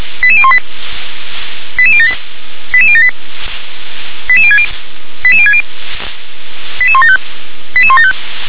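Five-tone selective call (ZVEI), heard as quick runs of five tones, tones of about 70 ms. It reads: 88018 88078 88078 88060 88060 88175 88175
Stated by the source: ZVEI five-tone call 88185 received by SDR.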